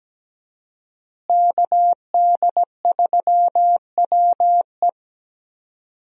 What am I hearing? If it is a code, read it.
Morse code, heard as a keyed tone, about 17 wpm, 700 Hz. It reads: KD3WE
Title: USA Call Signs - 5 Characters - Repeated 3x - 17wpm